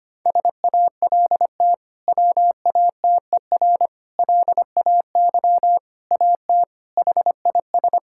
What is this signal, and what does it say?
Morse code, heard as a keyed tone, about 25 wpm, 700 Hz. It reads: SALT WATER LAY AT HIS
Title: Sentences from Top 1000 Words - Review Part 5 - 25wpm